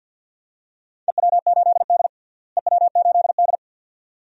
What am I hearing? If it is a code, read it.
Morse code, heard as a keyed tone, about 47 wpm, 700 Hz. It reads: EW8D EW8D
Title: DXCC-100 Call Signs - 4 Characters - Repeated 2x - 50wpm